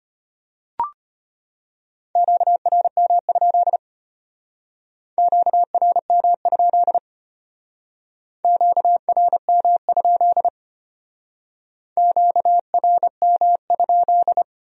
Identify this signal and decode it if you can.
Morse code, heard as a keyed tone, about 32 wpm, 700 Hz. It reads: QRM? QRM? QRM? QRM?